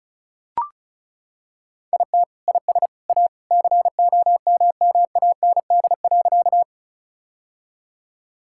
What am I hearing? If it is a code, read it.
Morse code, heard as a keyed tone, about 36 wpm, 700 Hz. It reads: IT IS A COMMAND.
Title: Sentences from Top 1600 Words 35wpm